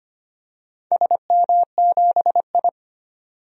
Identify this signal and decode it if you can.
Morse code, heard as a keyed tone, about 25 wpm, 700 Hz. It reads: SM7I
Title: DXCC-100 Call Signs - 4 Characters - 25wpm